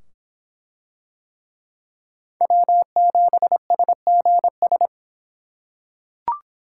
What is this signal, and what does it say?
Morse code, heard as a keyed tone, about 22 wpm, 700 Hz. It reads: W7SGS